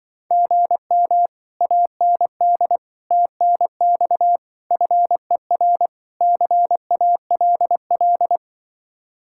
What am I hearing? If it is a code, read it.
Morse code, heard as a keyed tone, about 23 wpm, 700 Hz. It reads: GM AND TNX FER CALL
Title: Speed-Racing Common QSO Phrases - 20wpm